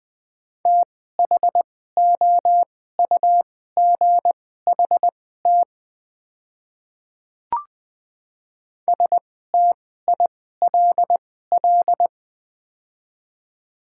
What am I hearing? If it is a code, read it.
Morse code, heard as a keyed tone, about 20 wpm, spaced as for 15 wpm, 700 Hz. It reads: THOUGHT STILL
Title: 2x Character Spacing - Top 300 Words - 20wpm